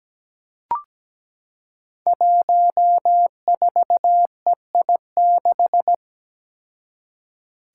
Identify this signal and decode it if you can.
Morse code, heard as a keyed tone, about 17 wpm, 700 Hz. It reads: 14EI6
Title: Five Challenging Letter-Number Combinations 17wpm